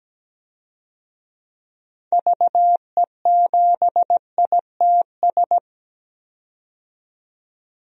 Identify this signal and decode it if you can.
Morse code, heard as a keyed tone, about 17 wpm, 700 Hz. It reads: VE7ITS
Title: DXCC-100 Call Signs - 6 Characters - 17wpm